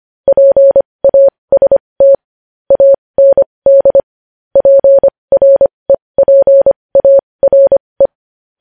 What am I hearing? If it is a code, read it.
Morse code, heard as a keyed tone, about 25 wpm, spaced as for 20 wpm, 550 Hz. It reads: PAST AND PREPARE
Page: news-headlines-in-morse-code-at-20-wpm